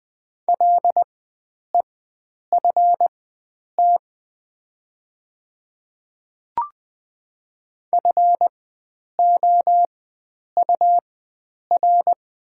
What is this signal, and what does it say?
Morse code, heard as a keyed tone, about 20 wpm, spaced as for 9 wpm, 700 Hz. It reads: LEFT FOUR
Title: 4x Character Spacing - Top 300 Words - 20wpm